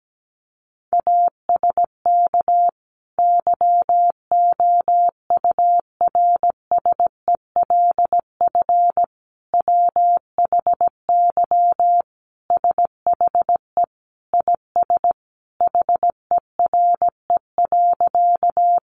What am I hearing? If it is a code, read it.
Morse code, heard as a keyed tone, about 17 wpm, 700 Hz. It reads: ASK YOURSELF WHY SHE IS HERE.
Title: Sentences from Top 600 Words 17wpm